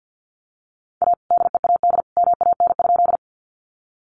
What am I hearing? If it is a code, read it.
Morse code, heard as a keyed tone, about 50 wpm, 700 Hz. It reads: A DEAD MAN?
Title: Sentences from Top 1500 Words 50wpm